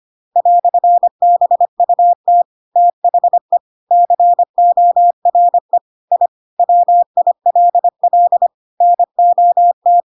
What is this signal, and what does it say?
Morse code, heard as a keyed tone, about 25 wpm, 700 Hz. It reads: "BUT THE CORE I WILL NOT
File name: Pinocchio / Pinocchio ch7 25 25 ik1hge